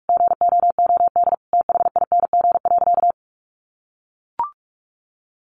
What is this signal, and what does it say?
Morse code, heard as a keyed tone, about 44 wpm, 700 Hz. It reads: GOOD THING.
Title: Sentences from Top 1500 Words 45wpm